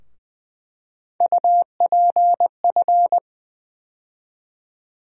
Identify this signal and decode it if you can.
Morse code, heard as a keyed tone, about 20 wpm, 700 Hz. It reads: UPF
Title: USA Call Sign Postfixes - 3 Letters 20wpm